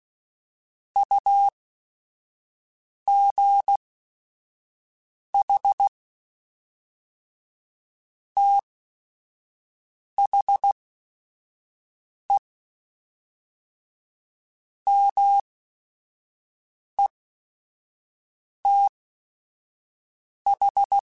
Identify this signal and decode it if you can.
Morse code, heard as a keyed tone, about 16 wpm, spaced as for 5 wpm, 800 Hz. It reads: UGH THE METH
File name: Technical Subjects / Technical Subjects-mp3s / AND Gate @5